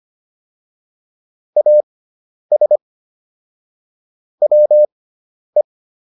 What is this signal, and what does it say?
Morse code, heard as a keyed tone, about 25 wpm, spaced as for 10 wpm, 600 Hz. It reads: AS WE